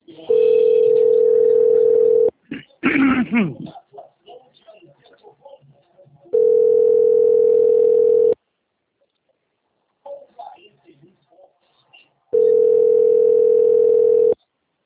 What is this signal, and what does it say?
Ringback tone (North America), 440 + 480 Hz.